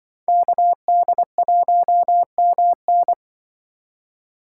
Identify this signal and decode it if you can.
Morse code, heard as a keyed tone, about 24 wpm, 700 Hz. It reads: KD1MN
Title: Speed Racing - USA Call Signs 20wpm